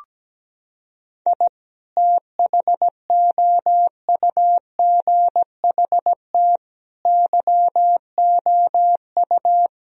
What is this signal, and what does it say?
Morse code, heard as a keyed tone, about 17 wpm, 700 Hz. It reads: I THOUGHT YOU
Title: Sentences from Top 1000 Words - Review Part 4 - 17wpm